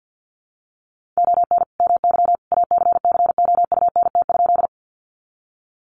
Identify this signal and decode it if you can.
Morse code, heard as a keyed tone, about 49 wpm, 700 Hz. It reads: ON MY ACCOUNT?
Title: Sentences from Top 900 Words 50wpm